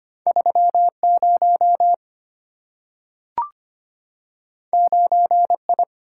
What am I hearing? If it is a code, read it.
Morse code, heard as a keyed tone, about 25 wpm, 700 Hz. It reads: 3 0 9 I
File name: Two Letter-Number Combinations - 25wpm